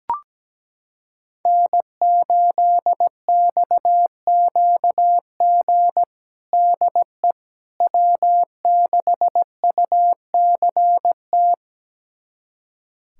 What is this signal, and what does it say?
Morse code, heard as a keyed tone, about 17 wpm, 700 Hz. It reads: N8XQG DE W6UCT
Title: USA Callsign DE Callsign - 5 Characters - 17wpm